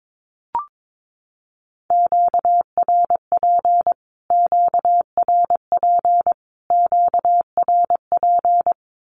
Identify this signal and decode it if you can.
Morse code, heard as a keyed tone, about 22 wpm, 700 Hz. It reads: QRP QRP QRP
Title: Q-Codes Repeated 3x 22wpm